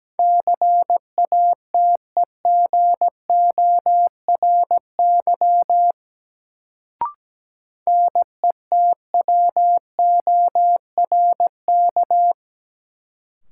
Morse code, heard as a keyed tone, about 17 wpm, 700 Hz. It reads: CATEGORY NETWORK
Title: Sentences from Top 1000 Words - Review Part 7 - New Word Drill - 17wpm